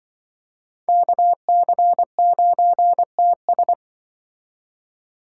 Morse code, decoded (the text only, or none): KC9TS